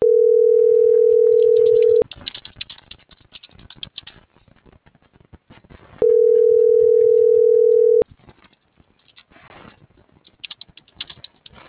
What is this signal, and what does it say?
Ringback tone (North America), 440 + 480 Hz.